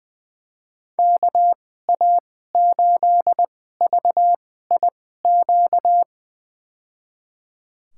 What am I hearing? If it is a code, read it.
Morse code, heard as a keyed tone, about 20 wpm, spaced as for 14 wpm, 700 Hz. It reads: KA8VIQ
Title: USA Call Signs - 6 Characters - 2x Character Spacing - 20wpm